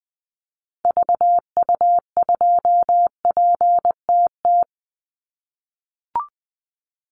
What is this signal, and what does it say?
Morse code, heard as a keyed tone, about 20 wpm, 700 Hz. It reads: VU2PTT
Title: Call Signs - 20wpm